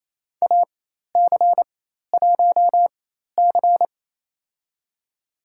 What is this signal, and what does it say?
Morse code, heard as a keyed tone, about 28 wpm, spaced as for 13 wpm, 700 Hz. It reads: AC1C
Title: USA Call Signs - 4 Characters - 4x Character Spacing - 28wpm